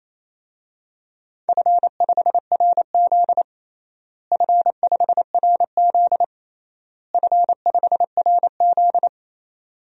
Morse code, heard as a keyed tone, about 27 wpm, 700 Hz. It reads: F5RZ F5RZ F5RZ
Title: DXCC-100 Call Signs - 4 Characters - Repeated 3x - 28wpm